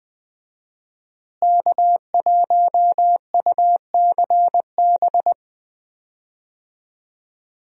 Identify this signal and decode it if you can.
Morse code, heard as a keyed tone, about 20 wpm, 700 Hz. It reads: K1UCB
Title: USA Call Signs - 5 Characters - Repeated 2x - 20wpm